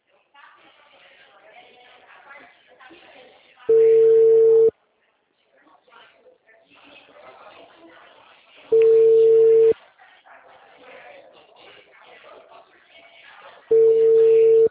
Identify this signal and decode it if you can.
Ringback tone (China, 450 Hz), with speech.